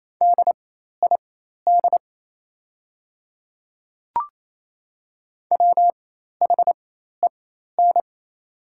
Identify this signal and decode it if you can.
Morse code, heard as a keyed tone, about 27 wpm, spaced as for 13 wpm, 700 Hz. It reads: DID WHEN
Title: 4x Character Spacing - Top 500 Words - 28wpm